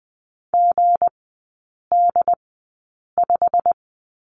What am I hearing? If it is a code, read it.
Morse code, heard as a keyed tone, about 20 wpm, spaced as for 8 wpm, 690 Hz. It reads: GD5
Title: 159 Lesson 26e - Letter g - Three Characters 20wpm